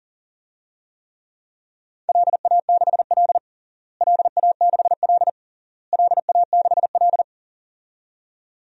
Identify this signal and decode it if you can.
Morse code, heard as a keyed tone, about 40 wpm, 700 Hz. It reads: LA6L LA6L LA6L
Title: DXCC-100 Call Signs - 4 Characters - Repeated 3x - 40wpm